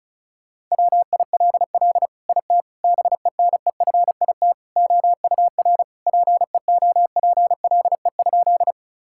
Morse code, heard as a keyed tone, about 36 wpm, 700 Hz. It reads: WILL IT BENEFIT OUR PEOPLE?
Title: Sentences from Top 1000 Words - Review Part 6 - 35wpm